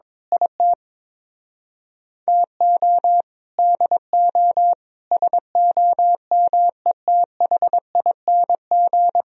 Morse code, heard as a keyed tone, about 21 wpm, 700 Hz. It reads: IT TO DO SOMETHING
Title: Speed-Racing - Sets of 5 Words 22wpm